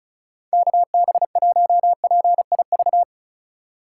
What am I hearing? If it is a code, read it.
Morse code, heard as a keyed tone, about 34 wpm, 700 Hz. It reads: KB1PIV
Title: USA Call Signs - 3x Character Spacing - 35wpm